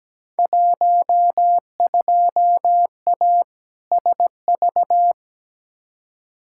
Morse code, heard as a keyed tone, about 17 wpm, 700 Hz. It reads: 12A SV
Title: Field Day Exchanges Encore - 17wpm